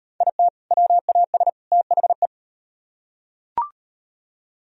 Morse code, heard as a keyed tone, about 38 wpm, 700 Hz. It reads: IT WAS THE